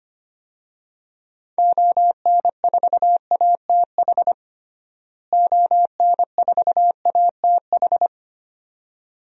Morse code, heard as a keyed tone, about 25 wpm, 700 Hz. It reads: ON4ATH ON4ATH
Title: DXCC-100 Call Signs - 6 Characters - Repeated 2x - 25wpm